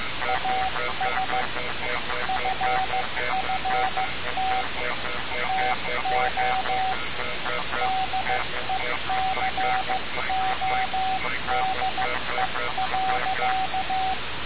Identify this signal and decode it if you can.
Morse code, heard as a keyed tone, about 15 wpm, 760 Hz, with noise. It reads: AS FFT TO DECODE.
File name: radiocap / REC z